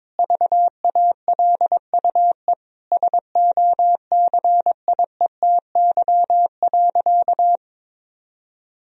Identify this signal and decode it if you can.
Morse code, heard as a keyed tone, about 21 wpm, 700 Hz. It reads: VALUE SOCIETY.
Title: Sentences from Top 1000 Words - Review Part 4 - 22wpm